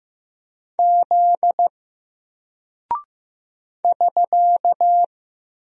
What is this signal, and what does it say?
Morse code, heard as a keyed tone, about 15 wpm, 700 Hz. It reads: Z <SK>
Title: Single Character (Letter:Number:Punctuation:Prosign) Alt 15wpm